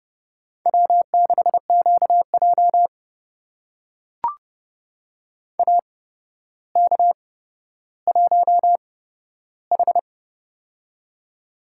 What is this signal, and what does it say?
Morse code, heard as a keyed tone, about 30 wpm, spaced as for 13 wpm, 700 Hz. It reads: W6QJ AK1H
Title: USA Call Signs - 8x Character Spacing - 30wpm